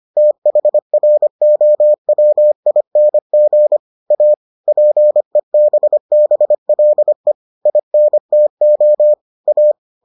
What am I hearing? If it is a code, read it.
Morse code, heard as a keyed tone, about 25 wpm, 600 Hz. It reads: THROWING A PEBBLE INTO A